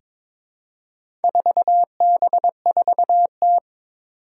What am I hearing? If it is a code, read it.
Morse code, heard as a keyed tone, about 22 wpm, 700 Hz. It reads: 4B4T
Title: Four Challenging Letter-Number Combinations 22wpm